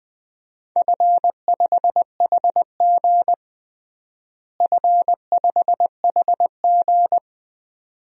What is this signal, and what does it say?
Morse code, heard as a keyed tone, about 21 wpm, 700 Hz. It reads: F5HG F5HG
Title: DXCC-100 Call Signs - 4 Characters - Repeated 2x - 20wpm